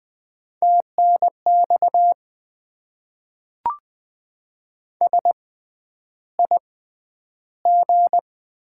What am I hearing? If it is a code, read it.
Morse code, heard as a keyed tone, about 20 wpm, spaced as for 10 wpm, 700 Hz. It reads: TNX SIG
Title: Top 90 QSO Elements - 6x Character Spacing - 20wpm